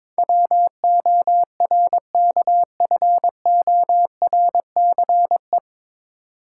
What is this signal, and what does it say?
Morse code, heard as a keyed tone, about 22 wpm, 700 Hz. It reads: WORKFORCE
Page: Morse Code Ninja practice 22wpm